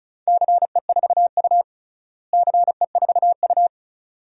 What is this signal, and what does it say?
Morse code, heard as a keyed tone, about 36 wpm, 700 Hz. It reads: CE4U CE4U
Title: DXCC-100 Call Signs - 4 Characters - Repeated 2x - 35wpm